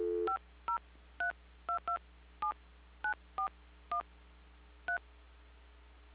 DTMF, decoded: 5032279413